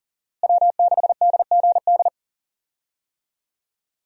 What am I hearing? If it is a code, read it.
Morse code, heard as a keyed tone, about 40 wpm, 700 Hz. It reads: W6DGD
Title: USA Call Signs - 5 Characters - Repeated 3x - 40wpm